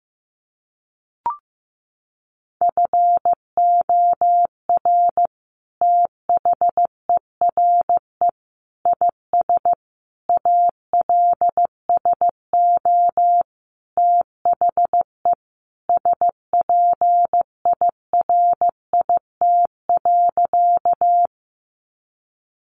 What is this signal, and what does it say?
Morse code, heard as a keyed tone, about 15 wpm, 700 Hz. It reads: FOR THERE IS ALSO THE SPIRIT.
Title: Sentences from Top 1400 Words 15wpm